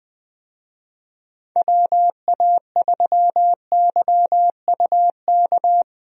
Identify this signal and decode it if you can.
Morse code, heard as a keyed tone, about 20 wpm, 700 Hz. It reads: WA3YUK
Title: USA Call Signs - 6 Characters - Repeated 2x - 20wpm